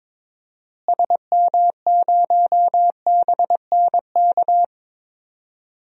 Morse code, heard as a keyed tone, about 22 wpm, 700 Hz. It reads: SM0BNK
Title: DXCC-100 Call Signs - 6 Characters - Repeated 3x - 22wpm